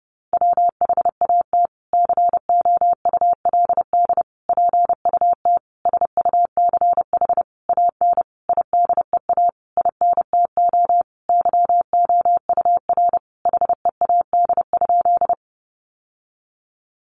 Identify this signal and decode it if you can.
Morse code, heard as a keyed tone, about 30 wpm, 700 Hz. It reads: WHAT COULD PUT SUCH AN IDEA INTO YOUR HEAD?
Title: Sentences from Top 600 Words 30wpm